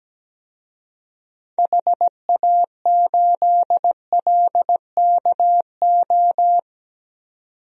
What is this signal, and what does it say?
Morse code, heard as a keyed tone, about 17 wpm, 700 Hz. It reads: HA8LKO
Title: DXCC-100 Call Signs - 6 Characters - 17wpm